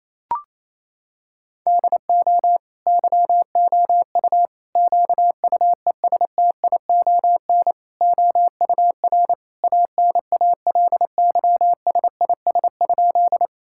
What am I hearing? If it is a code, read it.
Morse code, heard as a keyed tone, about 28 wpm, 700 Hz. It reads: DO YOU QUESTION OUR ANALYSIS?
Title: Sentences from Top 1000 Words - Review Part 5 - 28wpm